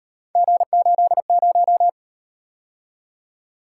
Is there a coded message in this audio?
Yes: Morse code from 0.4 to 1.9 s, heard as a keyed tone, about 37 wpm, 700 Hz. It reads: G80